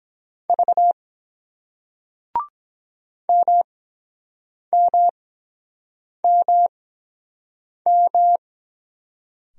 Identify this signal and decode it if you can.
Morse code, heard as a keyed tone, about 22 wpm, spaced as for 7 wpm, 700 Hz. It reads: V MMMM